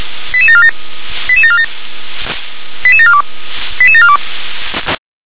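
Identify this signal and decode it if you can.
Five-tone selective call (ZVEI), heard as quick runs of five tones, tones of about 70 ms. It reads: 88747 88747 88742 88742
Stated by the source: ZVEI five-tone selective call 88088 received on SDR.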